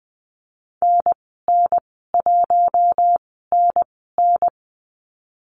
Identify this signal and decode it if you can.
Morse code, heard as a keyed tone, about 20 wpm, spaced as for 14 wpm, 700 Hz. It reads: NN1NN
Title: 048 Lesson 9g - Number 4 - Call Signs 20wpm